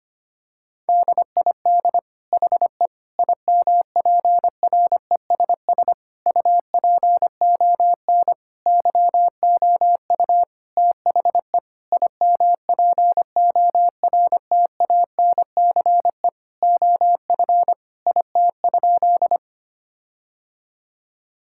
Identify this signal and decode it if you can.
Morse code, heard as a keyed tone, about 25 wpm, 700 Hz. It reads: DID HE IMPRESS UPON YOU THE IMPORTANCE OF IT?
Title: Sentences from Top 1600 Words 25wpm